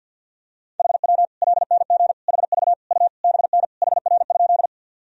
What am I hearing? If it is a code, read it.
Morse code, heard as a keyed tone, about 46 wpm, 700 Hz. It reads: HW LNG HV U BN HR?
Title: Sentences from Top 200 Words and Abbreviations 50wpm